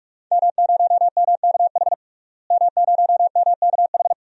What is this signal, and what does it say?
Morse code, heard as a keyed tone, about 41 wpm, 700 Hz. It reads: M0MKH M0MKH
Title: DXCC-100 Call Signs - 5 Characters - Repeated 2x - 45wpm